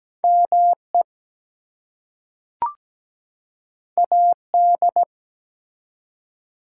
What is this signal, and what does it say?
Morse code, heard as a keyed tone, about 17 wpm, 700 Hz. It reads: ME AD
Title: Two Letter Words - 17wpm